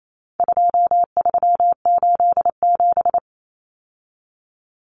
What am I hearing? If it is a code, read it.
Morse code, heard as a keyed tone, about 28 wpm, 700 Hz. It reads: 2387
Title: Common Cut Numbers - 4-Digit 28wpm